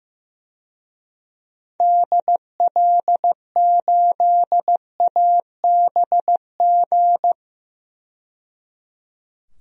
Morse code, heard as a keyed tone, about 15 wpm, 700 Hz. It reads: DL8ABG
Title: DXCC-100 Call Signs - 6 Characters - 15wpm